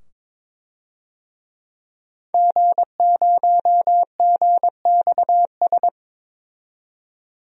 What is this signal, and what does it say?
Morse code, heard as a keyed tone, about 22 wpm, 700 Hz. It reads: G0GXS
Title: DXCC-100 Call Signs - 5 Characters - 22wpm